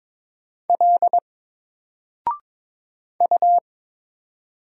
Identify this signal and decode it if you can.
Morse code, heard as a keyed tone, about 22 wpm, 700 Hz. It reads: L U
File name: Single-Letter-Number-22wpm